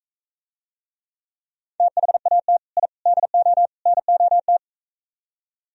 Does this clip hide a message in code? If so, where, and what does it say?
Yes: Morse code from 1.8 to 4.6 s, heard as a keyed tone, about 45 wpm, 700 Hz. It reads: THAT I DO NOT